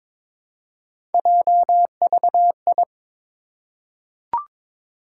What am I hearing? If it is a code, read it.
Morse code, heard as a keyed tone, about 22 wpm, 700 Hz. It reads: JVI
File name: Three Letter-Number Combinations - 22wpm